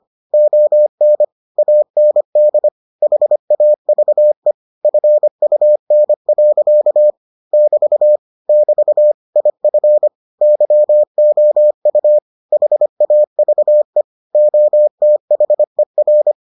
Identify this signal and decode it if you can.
Morse code, heard as a keyed tone, about 25 wpm, 600 Hz. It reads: ON AND HAVE FUN. <BT> <BT> IF YOU HAVE OTHER